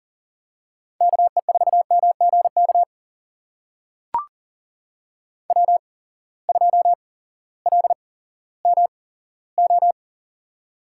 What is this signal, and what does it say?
Morse code, heard as a keyed tone, about 40 wpm, spaced as for 18 wpm, 700 Hz. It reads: KE4MGK W2LMO
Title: USA Call Signs - 8x Character Spacing - 40wpm